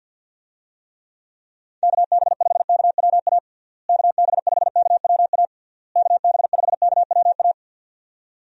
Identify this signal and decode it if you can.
Morse code, heard as a keyed tone, about 50 wpm, 700 Hz. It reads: KB5KWA KB5KWA KB5KWA
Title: USA Call Signs - 6 Characters - Repeated 3x - 50wpm